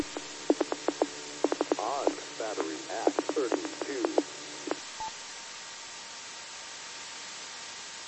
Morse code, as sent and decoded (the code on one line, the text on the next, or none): none